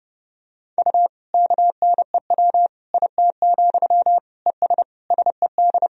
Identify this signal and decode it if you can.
Morse code, heard as a keyed tone, about 30 wpm, 700 Hz. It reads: U KNEW IT, ES SED
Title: Sentences from Top 500 Words and Abbreviations 30wpm